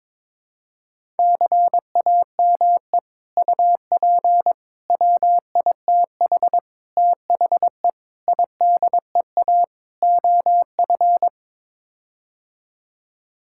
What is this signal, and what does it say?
Morse code, heard as a keyed tone, about 22 wpm, 700 Hz. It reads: CAME UP WITH THE IDEA OF
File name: practice / 22wpm / Sets of 6 Words - 22wpm